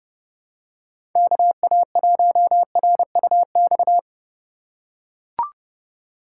Morse code, heard as a keyed tone, about 30 wpm, 700 Hz. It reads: KA1RUX
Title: USA Call Signs - 2x Character Spacing - 30wpm